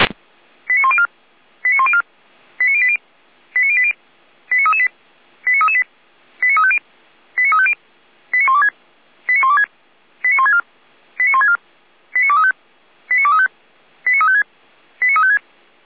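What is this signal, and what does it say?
Five-tone selective call (ZVEI), heard as quick runs of five tones, tones of about 70 ms. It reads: 89104 89104 89080 89080 89338 89338 89350 89350 89126 89126 89164 89164 89235 89235 89356 89356
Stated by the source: ZVEI five-tone call 89094 received on SDR.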